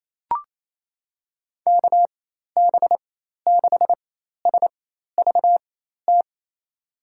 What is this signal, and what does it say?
Morse code, heard as a keyed tone, about 28 wpm, spaced as for 13 wpm, 700 Hz. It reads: KB6SVT
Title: USA Call Signs - 6 Characters - 4x Character Spacing - 28wpm